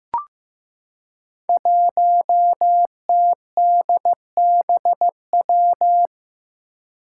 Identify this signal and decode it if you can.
Morse code, heard as a keyed tone, about 15 wpm, 700 Hz. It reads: 1TDBW
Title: Five Challenging Letter-Number Combinations 15wpm